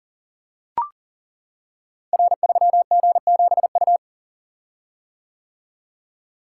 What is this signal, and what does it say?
Morse code, heard as a keyed tone, about 38 wpm, 700 Hz. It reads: R3G7U
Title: Code Groups - 5 Letter:Numbers - 40wpm